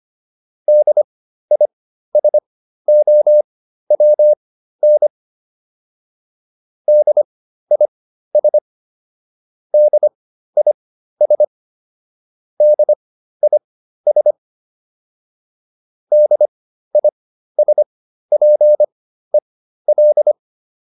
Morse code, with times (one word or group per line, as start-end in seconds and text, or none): 0.7-5.1 DISOWN
6.9-8.6 DIS
9.7-11.4 DIS
12.6-14.3 DIS
16.1-20.3 DISPEL